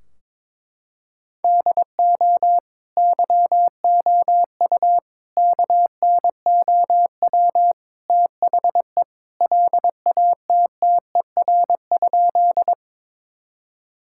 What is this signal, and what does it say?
Morse code, heard as a keyed tone, about 22 wpm, 700 Hz. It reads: DO YOU KNOW THE LATTER?
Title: Sentences from Top 1600 Words 22wpm